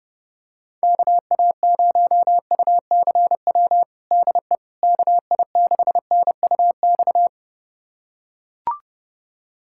Morse code, heard as a keyed tone, about 32 wpm, 700 Hz. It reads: KA0UCW DE KI6NUX
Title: USA Callsign DE Callsign - 6 Characters - 30wpm